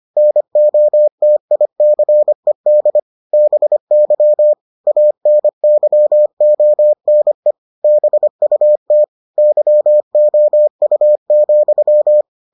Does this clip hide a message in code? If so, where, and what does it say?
Yes: Morse code from 0.2 to 12.2 s, heard as a keyed tone, about 25 wpm, 600 Hz. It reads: NOTICED BY ANYONE BUT YOU,